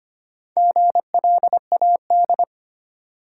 Morse code, heard as a keyed tone, about 25 wpm, 700 Hz. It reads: GLAD
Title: Sentences from Top 800 Words - New Word Drill - 25wpm